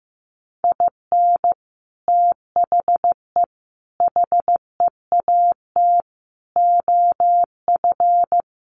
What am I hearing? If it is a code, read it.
Morse code, heard as a keyed tone, about 15 wpm, 700 Hz. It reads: IN THE HEAT OF
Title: Sets of 5 Words Encore2 15wpm